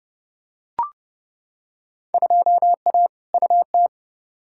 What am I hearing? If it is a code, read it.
Morse code, heard as a keyed tone, about 31 wpm, 700 Hz. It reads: 2A UT
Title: Field Day Exchanges - 30wpm